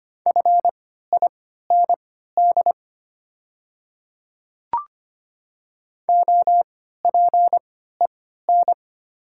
Morse code, heard as a keyed tone, about 25 wpm, spaced as for 14 wpm, 700 Hz. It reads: FIND OPEN
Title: 3x Character Spacing - Top 300 Words - 25wpm